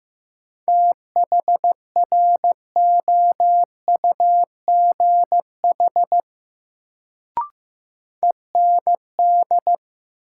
Morse code, heard as a keyed tone, about 15 wpm, 700 Hz. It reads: THROUGH END
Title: Sets of 2 Words - New Word Drill - 15wpm